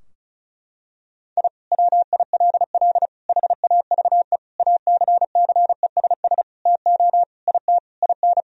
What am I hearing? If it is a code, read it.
Morse code, heard as a keyed tone, about 36 wpm, 700 Hz. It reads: I WILL HAVE ACCESS TO IT IN
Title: Sentences from Top 1000 Words - Review Part 4 - 35wpm